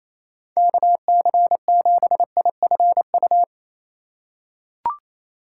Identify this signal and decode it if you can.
Morse code, heard as a keyed tone, about 28 wpm, 700 Hz. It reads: KC7IFU
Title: USA Call Signs - 6 Characters - 4x Character Spacing - 28wpm